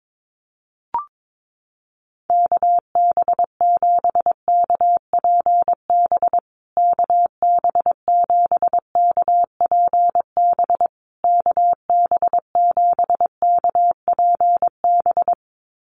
Morse code, heard as a keyed tone, about 22 wpm, 700 Hz. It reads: KB7KPB KB7KPB KB7KPB
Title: USA Call Signs Repeated 3x 22wpm